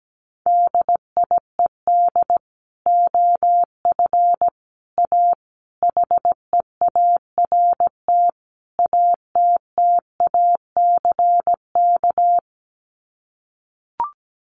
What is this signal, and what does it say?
Morse code, heard as a keyed tone, about 17 wpm, 700 Hz. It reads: DIED OF A HEART ATTACK
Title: Sets of 5 Words Encore2 17wpm